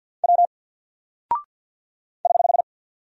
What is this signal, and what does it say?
Morse code, heard as a keyed tone, about 49 wpm, 700 Hz. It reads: W <HH>
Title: Single Character (Letter:Number:Punctuation:Prosign) Alt 50wpm